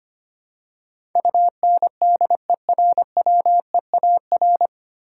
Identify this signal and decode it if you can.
Morse code, heard as a keyed tone, about 25 wpm, 700 Hz. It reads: UNDERWEAR
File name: Nine Letter Words - 25wpm